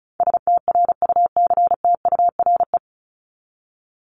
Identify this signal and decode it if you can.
Morse code, heard as a keyed tone, about 34 wpm, 700 Hz. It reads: STRUCTURE
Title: Top 751-1000 Words 35wpm